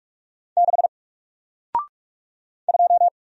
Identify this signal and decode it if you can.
Morse code, heard as a keyed tone, about 48 wpm, 700 Hz. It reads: 6 2
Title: Single Letter-Number Alt 45wpm